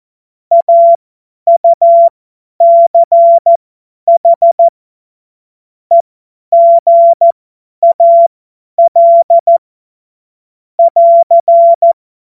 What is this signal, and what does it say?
Morse code, heard as a keyed tone, about 14 wpm, spaced as for 10 wpm, 680 Hz. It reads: AUCH EGAL <AR>
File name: Morsekurs / Witze / Witz 06. April 2021, 50 F70